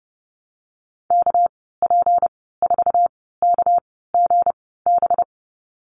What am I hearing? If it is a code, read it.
Morse code, heard as a keyed tone, about 30 wpm, spaced as for 17 wpm, 700 Hz. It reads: KP4KGB
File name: Lesson 37g - Letter x - Call Signs 30wpm